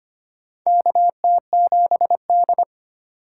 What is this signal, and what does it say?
Morse code, heard as a keyed tone, about 25 wpm, 700 Hz. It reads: KT7D
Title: USA Call Signs - 4 Characters - 2x Character Spacing - 25wpm